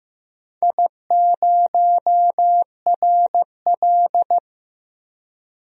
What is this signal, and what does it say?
Morse code, heard as a keyed tone, about 15 wpm, 700 Hz. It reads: I0RL
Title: DXCC-100 Call Signs - 4 Characters - 15wpm